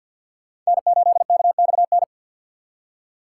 Morse code, heard as a keyed tone, about 64 wpm, 700 Hz. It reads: N8KXN